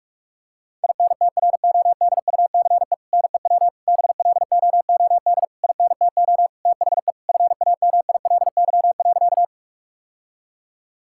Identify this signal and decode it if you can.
Morse code, heard as a keyed tone, about 46 wpm, 700 Hz. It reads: INTRODUCE NEW BLOOD INTO THE FAMILY.
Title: Sentences from Top 1600 Words 45wpm